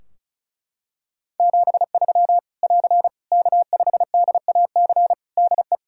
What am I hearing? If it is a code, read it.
Morse code, heard as a keyed tone, about 36 wpm, 700 Hz. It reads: 73 <AR> K5DAC DE
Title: Practice QSO - Exchange 4 - Ending - 35wpm